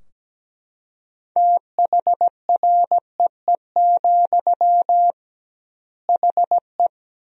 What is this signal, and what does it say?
Morse code, heard as a keyed tone, about 17 wpm, 700 Hz. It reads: THREE, HE
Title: 2x Word-Spacing - Sentences from Top 200 Words - 17wpm